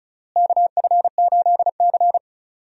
Morse code, heard as a keyed tone, about 34 wpm, 700 Hz. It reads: KF8C
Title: USA Call Signs - 4 Characters - Repeated 3x - 35wpm